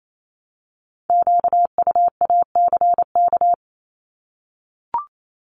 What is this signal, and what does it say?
Morse code, heard as a keyed tone, about 28 wpm, 700 Hz. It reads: QUACK
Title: Multi-letter Phonemes Repeated 3x 28wpm